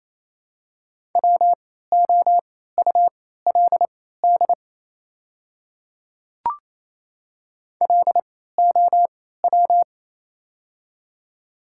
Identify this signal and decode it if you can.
Morse code, heard as a keyed tone, about 28 wpm, spaced as for 16 wpm, 700 Hz. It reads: WOULD LOW
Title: 3x Character Spacing - Top 500 Words - 28wpm